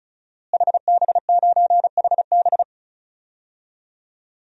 Morse code, heard as a keyed tone, about 36 wpm, 700 Hz. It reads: HB9HB
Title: DXCC-100 Call Signs - 5 Characters - Repeated 2x - 35wpm